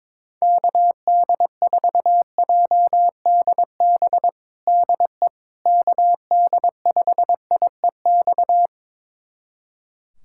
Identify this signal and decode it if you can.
Morse code, heard as a keyed tone, about 21 wpm, 700 Hz. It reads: KD4JDB DE KD5IEX
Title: USA Callsign DE Callsign 22wpm